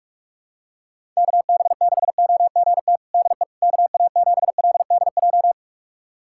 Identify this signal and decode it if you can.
Morse code, heard as a keyed tone, about 46 wpm, 700 Hz. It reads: KB6OGT DE KA7LDJ
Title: USA Callsign DE Callsign - 6 Characters - 45wpm